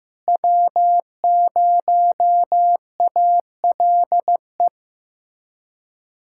Morse code, heard as a keyed tone, about 15 wpm, 700 Hz. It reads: W0ALE
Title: USA Callsign DE Callsign - 5 Characters - 15wpm